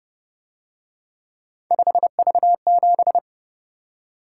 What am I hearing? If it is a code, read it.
Morse code, heard as a keyed tone, about 29 wpm, 700 Hz. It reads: 5V7
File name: Speed-Racing - Three Letter-Number Combinations - 25wpm